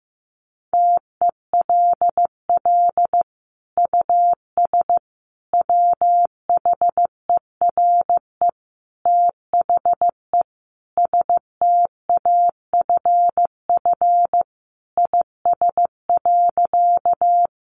Morse code, heard as a keyed tone, about 15 wpm, 700 Hz. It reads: TELL US WHERE THE STAFF IS.
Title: Sentences from Top 1200 Words 15wpm